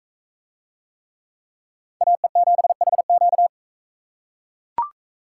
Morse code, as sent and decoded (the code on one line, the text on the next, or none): .- . --... .... --.-
AE7HQ